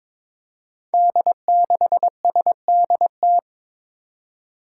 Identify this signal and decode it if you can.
Morse code, heard as a keyed tone, about 22 wpm, 700 Hz. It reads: D6SDT
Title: Five Challenging Letter-Number Combinations 22wpm